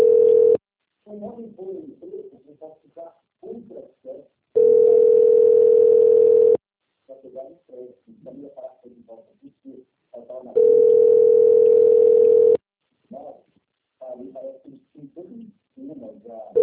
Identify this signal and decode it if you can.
Ringback tone (North America), 440 + 480 Hz, with speech.
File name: rg-1600-unknown- - - .20446